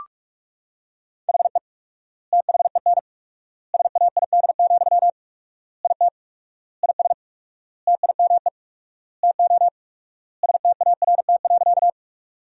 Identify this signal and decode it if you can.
Morse code, heard as a keyed tone, about 42 wpm, 700 Hz. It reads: HE THEN SAID, IT IS TIME TO START.
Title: 4x Word-Spacing - Sentences from Top 300 Words - 45wpm